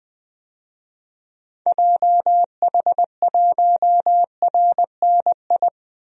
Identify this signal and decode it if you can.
Morse code, heard as a keyed tone, about 20 wpm, 700 Hz. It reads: JH1RNI
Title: DXCC-100 Call Signs - 6 Characters - 20wpm